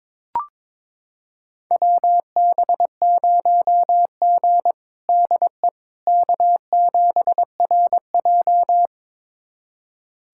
Morse code, heard as a keyed tone, about 21 wpm, 700 Hz. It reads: WB0G DE K7RJ
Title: USA Callsign DE Callsign - 4 Characters - 22wpm